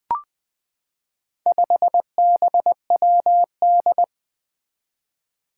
Morse code, heard as a keyed tone, about 20 wpm, 700 Hz. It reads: 5BWD
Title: Four Challenging Letter-Number Combinations 20wpm